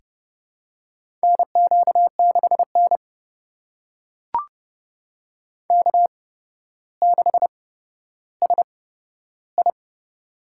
Morse code, heard as a keyed tone, about 30 wpm, spaced as for 13 wpm, 700 Hz. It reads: NQ6N K6SI
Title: USA Call Signs - 8x Character Spacing - 30wpm